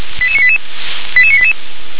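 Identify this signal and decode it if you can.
Five-tone selective call (ZVEI), heard as quick runs of five tones, tones of about 70 ms. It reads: 88988 88988
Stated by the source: ZVEI five-tone selective call received on an SDR.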